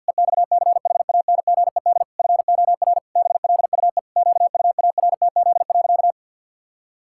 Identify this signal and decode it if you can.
Morse code, heard as a keyed tone, about 48 wpm, 700 Hz. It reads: EXCHANGED FOR BLUE QUARTZ.
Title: Pangram Challenge v2 - 50wpm